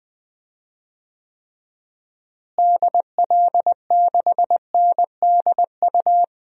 Morse code, heard as a keyed tone, about 20 wpm, 700 Hz. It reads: DL6NDU
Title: DXCC-100 Call Signs - 6 Characters - Repeated 2x - 20wpm